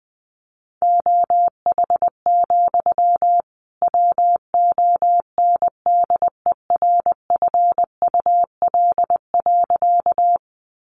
Morse code, heard as a keyed tone, about 20 wpm, 700 Hz. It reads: OH, WONDERFUL.
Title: Sentences from Top 1100 Words 20wpm